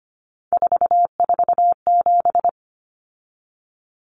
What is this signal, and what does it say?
Morse code, heard as a keyed tone, about 25 wpm, 700 Hz. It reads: 447
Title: Common Cut Numbers - 3-Digit 25wpm